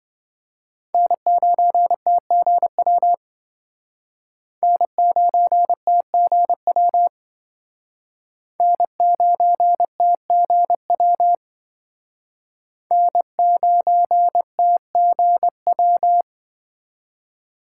Morse code, heard as a keyed tone, about 26 wpm, 700 Hz. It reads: N9TGW N9TGW N9TGW N9TGW